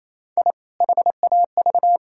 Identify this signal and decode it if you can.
Morse code, heard as a keyed tone, about 28 wpm, 700 Hz. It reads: I HAV
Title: Sentences from Top 1000 Words - Review Part 5 - 28wpm